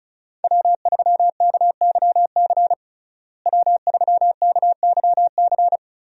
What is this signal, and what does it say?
Morse code, heard as a keyed tone, about 34 wpm, 700 Hz. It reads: W3KYC W3KYC
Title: USA Call Signs - 5 Characters - Repeated 2x - 35wpm